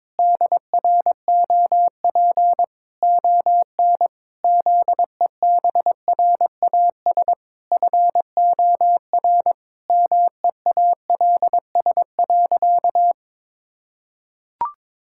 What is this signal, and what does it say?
Morse code, heard as a keyed tone, about 22 wpm, 700 Hz. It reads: DROP ON ZEBRAS FOR MEALS.
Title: Pangram Challenge v2 - 22wpm